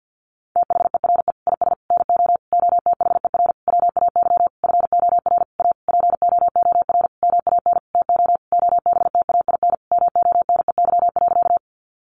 Morse code, heard as a keyed tone, about 50 wpm, 700 Hz. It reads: THERE IS NO OTHER WAY FOR A POOR MAN TO OBTAIN MONEY.
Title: Sentences from Top 1300 Words 50wpm